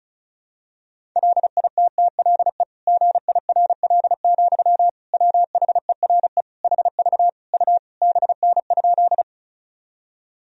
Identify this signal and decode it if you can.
Morse code, heard as a keyed tone, about 36 wpm, 700 Hz. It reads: LITTLE GIRL, WHERE HV U BN?
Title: Sentences from Top 400 Words and Abbreviations 35wpm